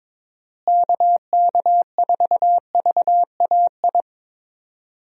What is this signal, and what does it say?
Morse code, heard as a keyed tone, about 23 wpm, 700 Hz. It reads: KK4VAI
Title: USA Call Signs - 6 Characters - Repeated 2x - 22wpm